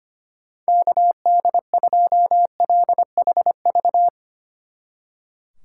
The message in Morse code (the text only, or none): KD2LHV